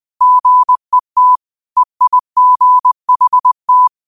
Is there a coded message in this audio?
Yes: Morse code from 0.2 to 3.9 s, heard as a keyed tone, about 20 wpm, 1000 Hz. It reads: GET EIGHT